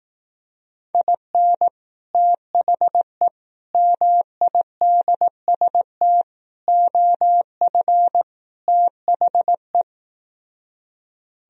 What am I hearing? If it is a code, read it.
Morse code, heard as a keyed tone, about 18 wpm, 700 Hz. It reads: IN THE MIDST OF THE
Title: Speed-Racing - Sets of 5 Words 15wpm